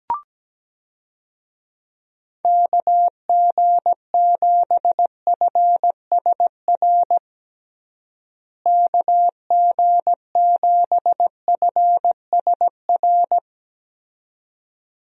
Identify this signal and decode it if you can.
Morse code, heard as a keyed tone, about 17 wpm, 700 Hz. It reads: KG7FSR KG7FSR
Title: USA Call Signs - 6 Characters - Repeated 2x - 17wpm